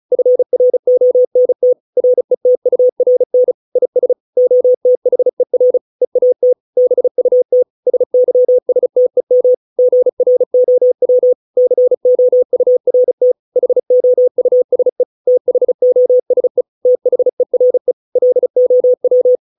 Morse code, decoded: FRONT RETURN IS OTHER EAT BUT SYSTEM GROW COURT HOUSE THOSE THERE LOW

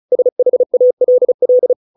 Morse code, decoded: SHALL

